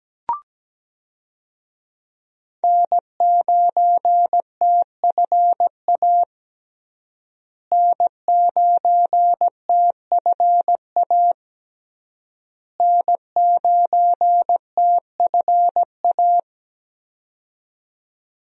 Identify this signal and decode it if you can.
Morse code, heard as a keyed tone, about 17 wpm, 700 Hz. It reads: N9TFA N9TFA N9TFA
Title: USA Call Signs - 5 Characters - Repeated 3x - 17wpm